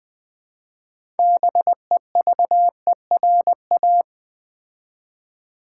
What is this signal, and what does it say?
Morse code, heard as a keyed tone, about 20 wpm, 700 Hz. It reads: BEVERA
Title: Build A Word - 2 Letters At A Time - 8-Letter Words - 20wpm